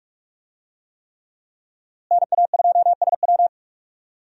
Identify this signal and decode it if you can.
Morse code, heard as a keyed tone, about 41 wpm, 700 Hz. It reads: NA2SW
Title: USA Call Signs - 5 Characters - Repeated 3x - 45wpm